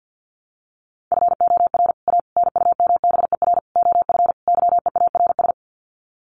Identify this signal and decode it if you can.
Morse code, heard as a keyed tone, about 48 wpm, 700 Hz. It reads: FOR A NUMBER OF YEARS
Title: Sets of 5 Words Encore 50wpm